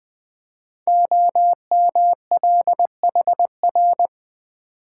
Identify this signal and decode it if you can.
Morse code, heard as a keyed tone, about 20 wpm, 700 Hz. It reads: OMLHR
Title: Code Groups - 5 Letters - 20wpm